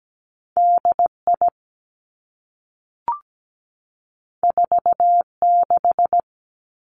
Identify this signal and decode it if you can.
Morse code, heard as a keyed tone, about 17 wpm, 700 Hz. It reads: DI 46
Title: Two Challenging Letter-Number Combinations 17wpm